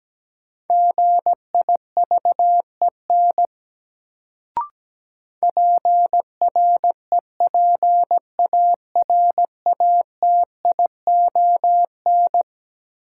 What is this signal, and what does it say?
Morse code, heard as a keyed tone, about 17 wpm, 700 Hz. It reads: GIVEN PREPARATION
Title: Sentences from Top 1400 Words - New Word Drill - 17wpm